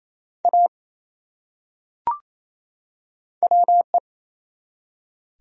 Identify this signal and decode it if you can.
Morse code, heard as a keyed tone, about 28 wpm, 700 Hz. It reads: A WE
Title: Top 100 Words 28wpm